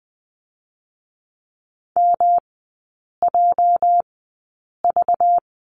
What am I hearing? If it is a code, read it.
Morse code, heard as a keyed tone, about 20 wpm, spaced as for 8 wpm, 700 Hz. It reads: MJV
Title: 218 Lesson 35e - Number 0 - Three Characters 20wpm